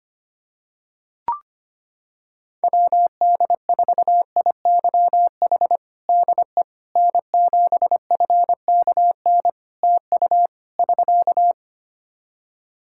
Morse code, heard as a keyed tone, about 25 wpm, 700 Hz. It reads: WD4IYH DE N7FKN TU <SK>
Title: Practice QSO - Exchange 4 - Ending - 25wpm